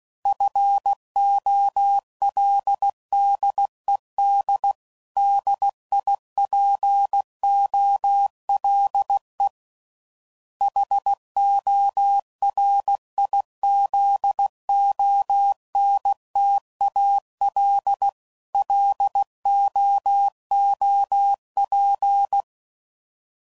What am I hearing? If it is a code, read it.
Morse code, heard as a keyed tone, about 16 wpm, 800 Hz. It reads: FOLDEDDIPOLE HORIZONTALLOOP
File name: CW Code Groups @15WPM / fromSM's / Antennas